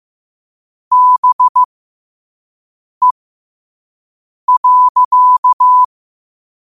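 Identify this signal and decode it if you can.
Morse code, heard as a keyed tone, about 15 wpm, spaced as for 5 wpm, 1000 Hz. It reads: BE.
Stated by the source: Morse at 5 wpm.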